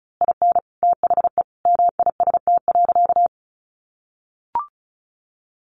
Morse code, heard as a keyed tone, about 34 wpm, 700 Hz. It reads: IN THE MIST.
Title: A Sound Of Thunder 35wpm